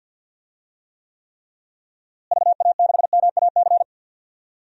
Morse code, heard as a keyed tone, about 52 wpm, 700 Hz. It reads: VA6MAC
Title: DXCC-100 Call Signs - 6 Characters - Repeated 3x - 50wpm